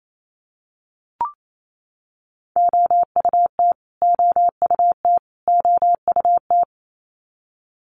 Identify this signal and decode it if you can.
Morse code, heard as a keyed tone, about 28 wpm, 700 Hz. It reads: OUT OUT OUT
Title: Top 300 Words Repeated 28wpm